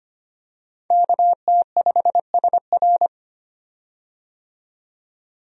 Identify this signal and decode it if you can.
Morse code, heard as a keyed tone, about 25 wpm, 700 Hz. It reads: KT5SR
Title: USA Call Signs - 5 Characters - 25wpm